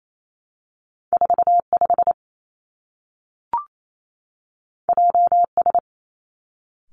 Morse code, heard as a keyed tone, about 28 wpm, 700 Hz. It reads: 4 5 J S
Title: Two Challenging Letter-Number Combinations 28wpm